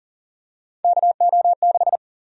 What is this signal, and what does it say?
Morse code, heard as a keyed tone, about 40 wpm, 700 Hz. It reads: KO6
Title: USA Call Sign Prefixes - 40wpm